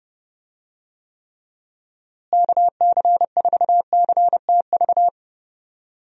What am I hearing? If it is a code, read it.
Morse code, heard as a keyed tone, about 30 wpm, 700 Hz. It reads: KC4CTV
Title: USA Call Signs - 6 Characters - Repeated 2x - 30wpm